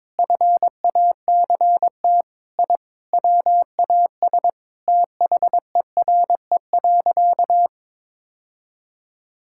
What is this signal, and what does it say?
Morse code, heard as a keyed tone, about 22 wpm, 700 Hz. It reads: FACT I WAS THERE.
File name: practice / 22wpm / Sentences with 4-Word Phrases - 22wpm